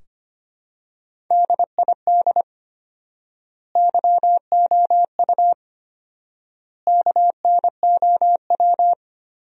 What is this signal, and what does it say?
Morse code, heard as a keyed tone, about 25 wpm, 700 Hz. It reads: DID YOU KNOW
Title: 4x Word-Spacing - Sentences from Top 100 Words - 25wpm